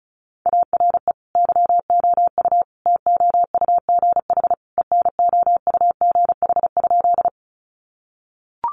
Morse code, heard as a keyed tone, about 34 wpm, 700 Hz. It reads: ARE YOU TOUGH ENOUGH?
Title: Sentences from Top 1200 Words 35wpm